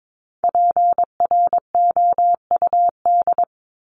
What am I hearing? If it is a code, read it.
Morse code, heard as a keyed tone, about 22 wpm, 700 Hz. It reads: PROUD
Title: Top 1501-1750 Words 22wpm